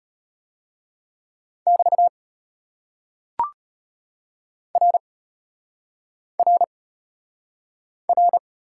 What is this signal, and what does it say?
Morse code, heard as a keyed tone, about 34 wpm, 700 Hz. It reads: <BT> R R R